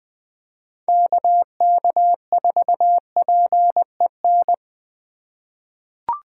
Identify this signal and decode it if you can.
Morse code, heard as a keyed tone, about 20 wpm, 700 Hz. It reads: KK4PEN
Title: USA Call Signs - 2x Character Spacing - 20wpm